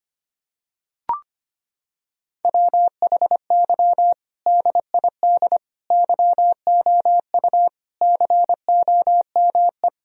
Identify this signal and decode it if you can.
Morse code, heard as a keyed tone, about 25 wpm, 700 Hz. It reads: WHY DID YOU COME
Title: Sentences from Top 1000 Words - Review Part 1 - 25wpm